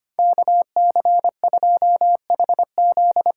Morse code, heard as a keyed tone, about 25 wpm, 700 Hz. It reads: KC2HZ
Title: USA Call Signs - 6 Characters - 8x Character Spacing - 25wpm